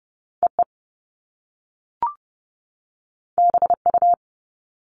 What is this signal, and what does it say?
Morse code, heard as a keyed tone, about 30 wpm, 700 Hz. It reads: EE BU
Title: Top Two Letter Combinations in Words 30wpm